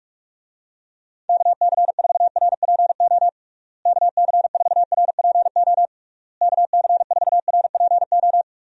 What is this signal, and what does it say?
Morse code, heard as a keyed tone, about 41 wpm, 700 Hz. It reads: KC4RPO KC4RPO KC4RPO
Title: USA Call Signs - 6 Characters - Repeated 3x - 45wpm